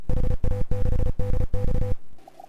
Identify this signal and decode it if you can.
Morse code, heard as a keyed tone, about 33 wpm, 500 Hz, with noise. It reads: HA6DX